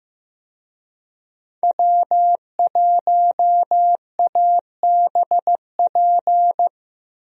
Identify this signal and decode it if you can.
Morse code, heard as a keyed tone, about 15 wpm, 700 Hz. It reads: W1ABP
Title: USA Call Signs - 5 Characters - Repeated 2x - 15wpm